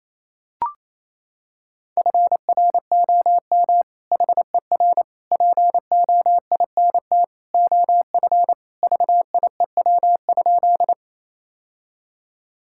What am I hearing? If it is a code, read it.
Morse code, heard as a keyed tone, about 28 wpm, 700 Hz. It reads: FROM HER POINT OF VIEW?
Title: Sentences from Top 1000 Words - Review Part 6 - 28wpm